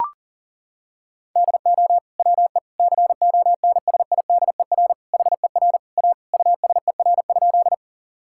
Morse code, heard as a keyed tone, about 40 wpm, 700 Hz. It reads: DO WE CONSIDER HER A USER?
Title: Sentences from Top 1000 Words - Review Part 5 - 40wpm